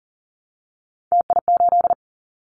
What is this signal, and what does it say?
Morse code, heard as a keyed tone, about 40 wpm, 700 Hz. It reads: TI8
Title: Callsign Prefixes for Top 100 DXCC - 40wpm